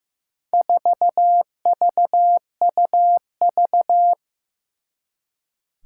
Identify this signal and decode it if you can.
Morse code, heard as a keyed tone, about 15 wpm, 700 Hz. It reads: 4VUV
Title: Four Challenging Letter-Number Combinations 15wpm